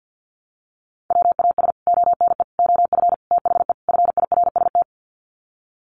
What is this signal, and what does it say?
Morse code, heard as a keyed tone, about 48 wpm, 700 Hz. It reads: WAS ONE OF THE FIRST
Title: Sets of 5 Words 50wpm